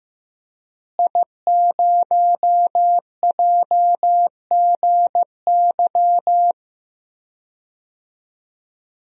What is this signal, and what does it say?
Morse code, heard as a keyed tone, about 16 wpm, 700 Hz. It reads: I0JGY
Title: DXCC-100 Call Signs - 5 Characters - Repeated 2x - 15wpm